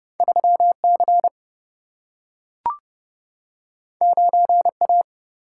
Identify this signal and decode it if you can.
Morse code, heard as a keyed tone, about 30 wpm, 700 Hz. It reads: 3 C 9 A
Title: Single-Letter Single-Number 30wpm